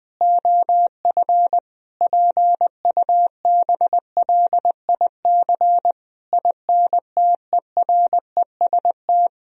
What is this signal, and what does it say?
Morse code, heard as a keyed tone, about 20 wpm, 700 Hz. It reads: OF PUBLIC INTEREST
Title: Sentences from Top 1000 Words - Review Part 3 - 20wpm